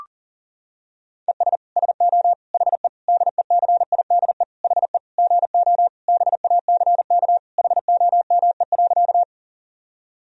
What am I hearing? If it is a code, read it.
Morse code, heard as a keyed tone, about 37 wpm, 700 Hz. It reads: ES SO HE DECIDE HE GO BACK HOME.
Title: Sentences from Top 500 Words and Abbreviations 40wpm